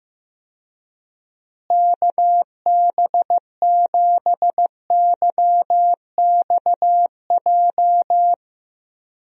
Morse code, heard as a keyed tone, about 15 wpm, 700 Hz. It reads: KB7YXJ
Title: USA Call Signs - 6 Characters - Repeated 2x - 15wpm